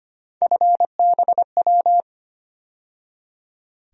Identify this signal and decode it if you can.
Morse code, heard as a keyed tone, about 25 wpm, 700 Hz. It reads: FBW
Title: USA Call Sign Postfixes - 3 Letters 25wpm